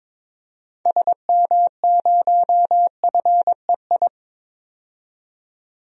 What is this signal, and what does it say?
Morse code, heard as a keyed tone, about 22 wpm, 700 Hz. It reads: SM0FEI
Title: DXCC-100 Call Signs - 6 Characters - 22wpm